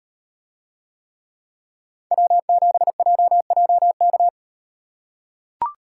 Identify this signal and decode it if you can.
Morse code, heard as a keyed tone, about 38 wpm, 700 Hz. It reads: W7JJK